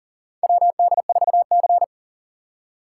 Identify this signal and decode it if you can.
Morse code, heard as a keyed tone, about 38 wpm, 700 Hz. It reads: WD4C
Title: USA Call Signs - 4 Characters - 2x Character Spacing - 40wpm